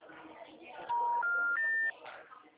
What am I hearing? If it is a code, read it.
Special information tone, with noise.